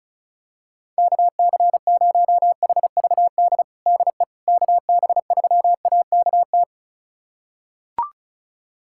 Morse code, heard as a keyed tone, about 34 wpm, 700 Hz. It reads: KC0HVD DE KB3AKT
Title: USA Callsign DE Callsign 35wpm